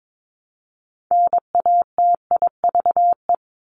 Morse code, heard as a keyed tone, about 22 wpm, 700 Hz. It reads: NATIVE
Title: Top 1751-2000 Words 22wpm